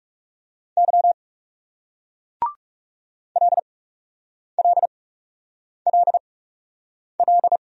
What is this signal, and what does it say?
Morse code, heard as a keyed tone, about 42 wpm, spaced as for 8 wpm, 700 Hz. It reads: Y LLLL